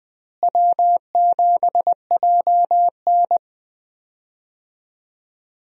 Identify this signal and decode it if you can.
Morse code, heard as a keyed tone, about 20 wpm, 700 Hz. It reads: W7JN
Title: USA Call Signs - 4 Characters - Repeated 3x - 20wpm